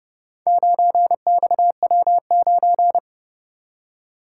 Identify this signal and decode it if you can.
Morse code, heard as a keyed tone, about 31 wpm, 700 Hz. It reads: 9XW9